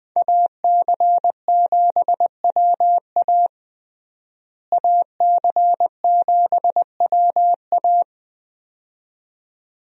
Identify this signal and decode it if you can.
Morse code, heard as a keyed tone, about 21 wpm, 700 Hz. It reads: AC7WA AC7WA
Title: USA Call Signs - 5 Characters - Repeated 2x - 20wpm